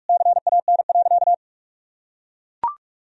Morse code, heard as a keyed tone, about 41 wpm, 700 Hz. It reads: CAN.
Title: Sentences from Top 1000 Words - Review Part 2 - 45wpm